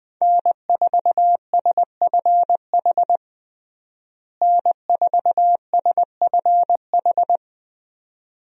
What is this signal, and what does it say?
Morse code, heard as a keyed tone, about 21 wpm, 700 Hz. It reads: N4SFH N4SFH
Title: USA Call Signs - 5 Characters - Repeated 2x - 20wpm